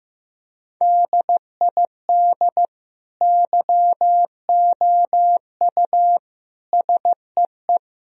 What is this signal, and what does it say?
Morse code, heard as a keyed tone, about 15 wpm, 700 Hz. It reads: DID YOU SEE
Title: Sentences from Top 1000 Words - Review Part 4 - 15wpm